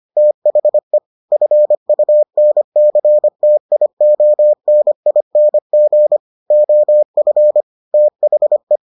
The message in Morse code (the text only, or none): THE FUNCTIONING OF THE